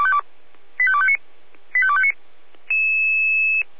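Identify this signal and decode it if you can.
Five-tone selective call (ZVEI), heard as quick runs of five tones, tones of about 70 ms. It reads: 85259 85259
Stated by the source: Five-tone ZVEI selective call received on SDR, fire brigade alerting.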